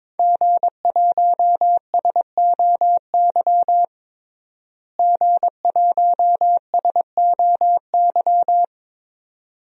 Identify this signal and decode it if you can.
Morse code, heard as a keyed tone, about 22 wpm, 700 Hz. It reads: G1SOY G1SOY
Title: DXCC-100 Call Signs - 5 Characters - Repeated 2x - 22wpm